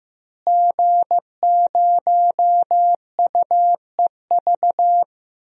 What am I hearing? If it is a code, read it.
Morse code, heard as a keyed tone, about 16 wpm, 700 Hz. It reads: G0UEV
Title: DXCC-100 Call Signs - 5 Characters - Repeated 3x - 15wpm